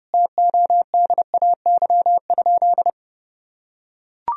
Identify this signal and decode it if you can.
Morse code, heard as a keyed tone, about 30 wpm, 700 Hz. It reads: TODAY?
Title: Sentences from Top 1000 Words - Review Part 2 - 30wpm